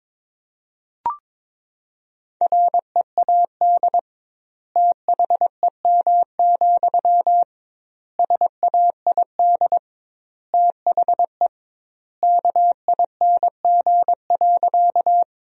Morse code, heard as a keyed tone, about 21 wpm, 700 Hz. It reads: READ THEM, SAID THE KING.
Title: 2x Word-Spacing - Sentences from Top 400 Words - 22wpm